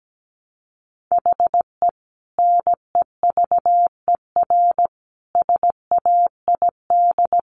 Morse code, heard as a keyed tone, about 17 wpm, 700 Hz. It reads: HE NEVER SAID
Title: Sentences from Top 500 Words 17wpm